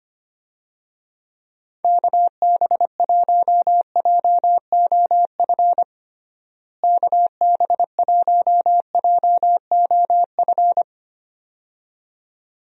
Morse code, heard as a keyed tone, about 25 wpm, 700 Hz. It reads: KB1JOF KB1JOF
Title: USA Call Signs - 6 Characters - Repeated 2x - 25wpm